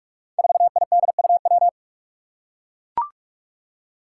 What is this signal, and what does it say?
Morse code, heard as a keyed tone, about 44 wpm, 700 Hz. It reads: 4IDUW
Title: Five Challenging Letter-Number Combinations 45wpm